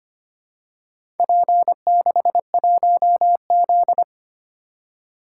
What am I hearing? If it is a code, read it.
Morse code, heard as a keyed tone, about 25 wpm, 700 Hz. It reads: P61Z
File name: practice / 25wpm / Four Letter-Number Combinations - 25wpm